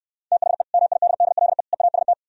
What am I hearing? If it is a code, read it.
Morse code, heard as a keyed tone, about 68 wpm, 700 Hz. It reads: THE MIDDLE EAST